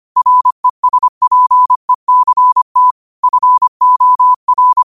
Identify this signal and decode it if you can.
Morse code, heard as a keyed tone, about 25 wpm, 1000 Hz. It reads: RESPECT FOR
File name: qotdfiles / 25wpm